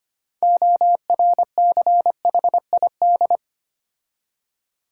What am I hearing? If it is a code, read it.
Morse code, heard as a keyed tone, about 25 wpm, 700 Hz. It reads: ORCHID
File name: practice / 25wpm / Difficult Words with Hints - 25wpm